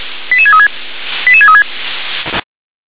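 Five-tone selective call (ZVEI), heard as quick runs of five tones, tones of about 70 ms. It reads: 88636 88636
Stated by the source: ZVEI five-tone call received on SDR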